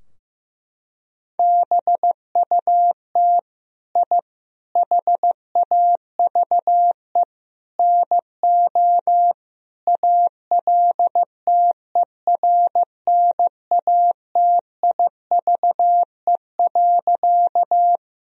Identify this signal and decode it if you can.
Morse code, heard as a keyed tone, about 16 wpm, 700 Hz. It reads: BUT I HAVE NO ALTERNATIVE.
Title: Sentences from Top 1000 Words - Review Part 4 - 15wpm